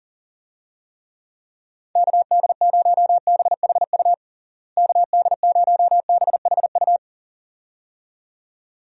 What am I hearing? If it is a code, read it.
Morse code, heard as a keyed tone, about 37 wpm, 700 Hz. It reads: KD0BHU KD0BHU
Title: USA Call Signs - 6 Characters - Repeated 2x - 40wpm